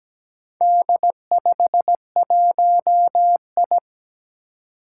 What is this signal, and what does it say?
Morse code, heard as a keyed tone, about 17 wpm, 700 Hz. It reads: D51I
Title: Four Challenging Letter-Number Combinations 17wpm